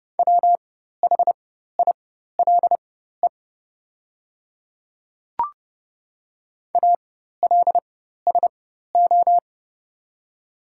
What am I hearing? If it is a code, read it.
Morse code, heard as a keyed tone, about 30 wpm, spaced as for 14 wpm, 700 Hz. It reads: WHILE ALSO
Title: 4x Character Spacing - Top 300 Words - 30wpm